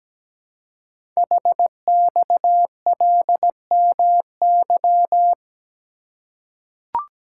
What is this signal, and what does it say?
Morse code, heard as a keyed tone, about 17 wpm, 700 Hz. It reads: HXLMY
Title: Code Groups - 5 Letters - 17wpm